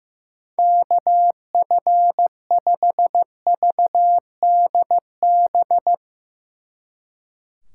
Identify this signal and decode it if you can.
Morse code, heard as a keyed tone, about 15 wpm, 700 Hz. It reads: KF5VDB
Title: Speed Racing - USA Call Signs 15wpm